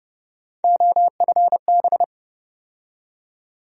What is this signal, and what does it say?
Morse code, heard as a keyed tone, about 30 wpm, 700 Hz. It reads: OFB
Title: USA Call Sign Postfixes - 3 Letters 30wpm